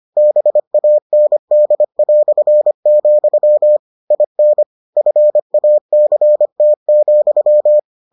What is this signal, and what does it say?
Morse code, heard as a keyed tone, about 25 wpm, 600 Hz. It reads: BAND", IN FACT,